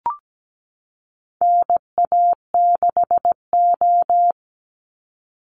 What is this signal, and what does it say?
Morse code, heard as a keyed tone, about 17 wpm, 700 Hz. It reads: NA6O
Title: NA SOTA Chaser Callsigns 17wpm